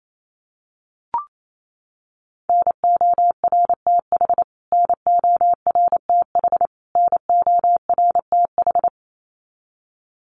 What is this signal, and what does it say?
Morse code, heard as a keyed tone, about 28 wpm, 700 Hz. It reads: NORTH NORTH NORTH
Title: Top 500 Words Repeated 28wpm